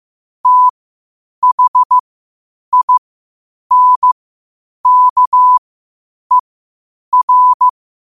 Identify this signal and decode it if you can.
Morse code, heard as a keyed tone, about 15 wpm, spaced as for 8 wpm, 1000 Hz. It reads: THINKER
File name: qotdfiles / 8wpm